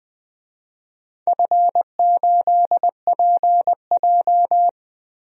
Morse code, heard as a keyed tone, about 20 wpm, 700 Hz. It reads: F8PJ
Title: DXCC-100 Call Signs - 4 Characters - Repeated 2x - 20wpm